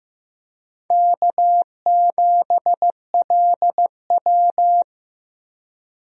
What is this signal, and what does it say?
Morse code, heard as a keyed tone, about 15 wpm, 700 Hz. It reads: K7LW
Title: USA Call Signs - 4 Characters - Repeated 3x - 15wpm